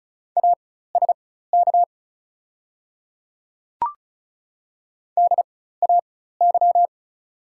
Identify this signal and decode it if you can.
Morse code, heard as a keyed tone, about 35 wpm, spaced as for 16 wpm, 690 Hz. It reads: ASK DAY
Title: 4x Character Spacing - Top 200 Words - 35wpm